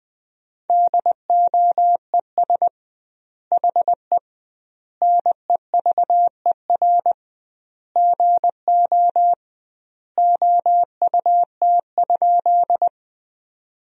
Morse code, heard as a keyed tone, about 20 wpm, 700 Hz. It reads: DOES HE NEVER GO OUT?
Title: 2x Word-Spacing - Sentences from Top 300 Words - 20wpm